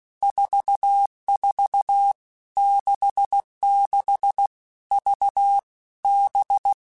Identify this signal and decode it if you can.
Morse code, heard as a keyed tone, about 16 wpm, 800 Hz. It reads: 44 66 V B
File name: Basic Letter Groups By7Groups / VB46 / VB46@15